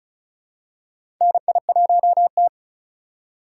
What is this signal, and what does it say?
Morse code, heard as a keyed tone, about 34 wpm, 700 Hz. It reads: NI1T
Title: USA Call Signs - 4 Characters - 4x Character Spacing - 35wpm